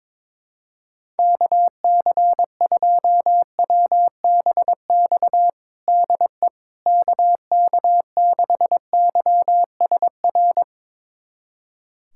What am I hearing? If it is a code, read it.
Morse code, heard as a keyed tone, about 22 wpm, 700 Hz. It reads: KC2WBX DE KK6YSR
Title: USA Callsign DE Callsign - 6 Characters - 22wpm